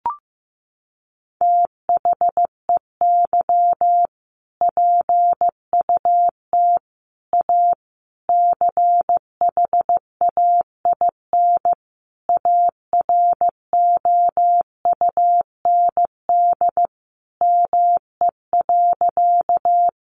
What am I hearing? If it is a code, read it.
Morse code, heard as a keyed tone, about 15 wpm, 700 Hz. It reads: THEY PUT A CHAIN AROUND ME.
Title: Sentences from Top 1500 Words 15wpm